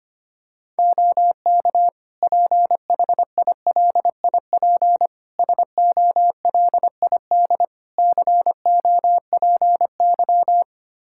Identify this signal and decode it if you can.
Morse code, heard as a keyed tone, about 25 wpm, 700 Hz. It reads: OK PHILIP SOLID COPY
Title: K1USN SST QSO Practice 25wpm